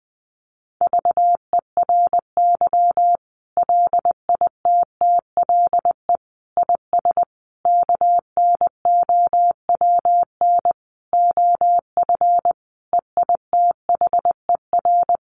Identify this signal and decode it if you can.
Morse code, heard as a keyed tone, about 20 wpm, 700 Hz. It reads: VERY LITTLE IS KNOWN OF EITHER
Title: Sentences from Top 1500 Words 20wpm